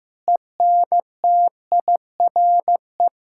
Morse code, heard as a keyed tone, about 16 wpm, 700 Hz. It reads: ENTIRE
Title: Sentences from Top 1000 Words - Review Part 5 - 15wpm